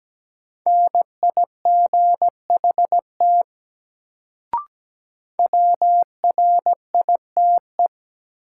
Morse code, heard as a keyed tone, about 17 wpm, 700 Hz. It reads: NIGHT WRITE
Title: Sentences from Top 300 Words - New Word Drill - 17wpm